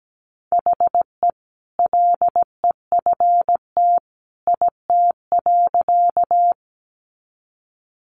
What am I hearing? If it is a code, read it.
Morse code, heard as a keyed tone, about 17 wpm, 700 Hz. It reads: HE LEFT IT.
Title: A Sound Of Thunder 17wpm